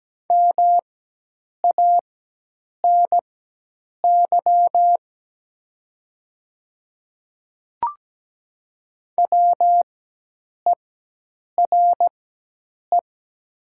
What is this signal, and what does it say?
Morse code, heard as a keyed tone, about 17 wpm, spaced as for 8 wpm, 700 Hz. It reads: MANY WERE
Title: 4x Character Spacing - Top 100 Words - 17wpm